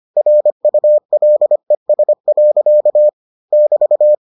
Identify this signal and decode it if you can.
Morse code, heard as a keyed tone, about 25 wpm, 600 Hz. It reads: RULES. <BT>